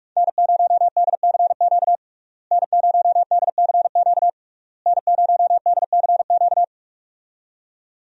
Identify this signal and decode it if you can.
Morse code, heard as a keyed tone, about 46 wpm, 700 Hz. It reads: N0DCQ N0DCQ N0DCQ
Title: USA Call Signs - 5 Characters - Repeated 3x - 45wpm